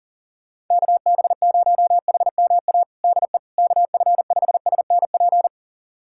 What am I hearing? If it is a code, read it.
Morse code, heard as a keyed tone, about 40 wpm, 700 Hz. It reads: KB0HMA DE KF5SNP
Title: USA Callsign DE Callsign - 6 Characters - 40wpm